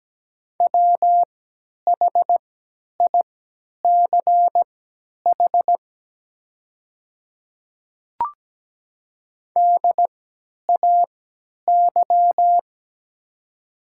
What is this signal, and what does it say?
Morse code, heard as a keyed tone, about 17 wpm, spaced as for 10 wpm, 700 Hz. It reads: WHICH DAY
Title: 3x Character Spacing - Top 100 Words - 17wpm